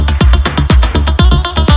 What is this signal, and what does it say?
There is no code.